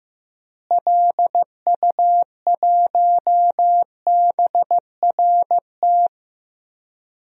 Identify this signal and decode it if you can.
Morse code, heard as a keyed tone, about 15 wpm, 700 Hz. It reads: LU1BRT
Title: DXCC-100 Call Signs - 6 Characters - Repeated 3x - 15wpm